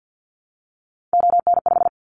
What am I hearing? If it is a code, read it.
Morse code, heard as a keyed tone, about 50 wpm, 700 Hz. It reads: ON5
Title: Callsign Prefixes for Top 100 DXCC - 50wpm